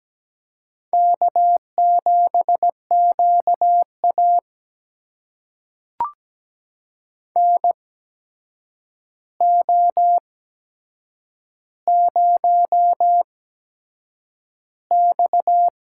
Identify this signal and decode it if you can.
Morse code, heard as a keyed tone, about 17 wpm, spaced as for 7 wpm, 700 Hz. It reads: K7QA NO0X
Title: USA Call Signs - 4 Characters - 8x Character Spacing - 17wpm